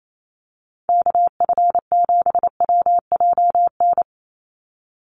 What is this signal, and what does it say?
Morse code, heard as a keyed tone, about 28 wpm, 700 Hz. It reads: KF7WJN
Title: USA Call Signs 28wpm